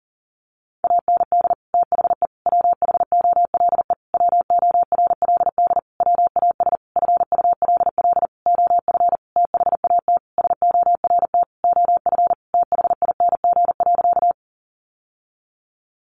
Morse code, heard as a keyed tone, about 40 wpm, 700 Hz. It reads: AND THE WHOLE WORLD WAS FULL OF THAT SORT OF THING.
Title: Sentences from Top 1300 Words 40wpm